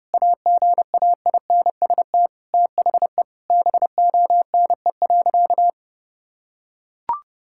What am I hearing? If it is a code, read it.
Morse code, heard as a keyed tone, about 30 wpm, 700 Hz. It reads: AGAINST THE BONE.
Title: Sentences from Top 1600 Words 30wpm